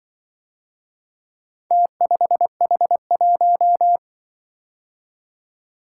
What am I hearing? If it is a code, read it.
Morse code, heard as a keyed tone, about 25 wpm, 700 Hz. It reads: T5H1
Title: Speed-Racing - Four Letter-Number Combinations - 20wpm